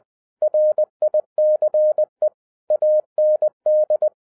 Morse code, heard as a keyed tone, about 20 wpm, 610 Hz. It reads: RICE AND